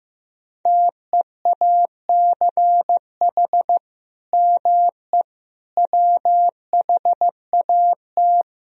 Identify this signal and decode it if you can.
Morse code, heard as a keyed tone, about 15 wpm, 700 Hz. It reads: TEACH ME WHAT
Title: Sentences from Top 500 Words and Abbreviations 15wpm